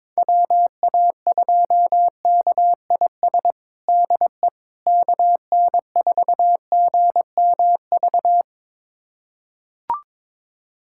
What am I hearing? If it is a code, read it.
Morse code, heard as a keyed tone, about 22 wpm, 700 Hz. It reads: WA2KIS DE KN4GMV
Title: USA Callsign DE Callsign 22wpm